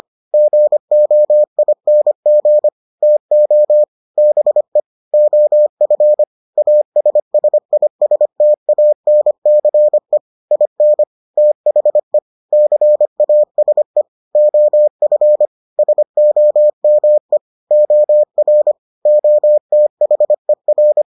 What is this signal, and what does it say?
Morse code, heard as a keyed tone, about 25 wpm, 600 Hz. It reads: GOING TO BE OF ASSISTANCE IN THE CASE OF SOME OR OTHER